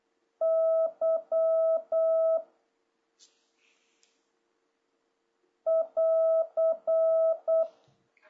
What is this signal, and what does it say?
Morse code, heard as a keyed tone, about 8 wpm, 1300 Hz. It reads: Y <AR>
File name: jh8cbh / 2026cw